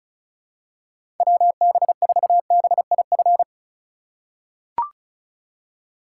Morse code, heard as a keyed tone, about 36 wpm, 690 Hz. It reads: WB4BIF
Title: USA Call Signs - 6 Characters - 4x Character Spacing - 35wpm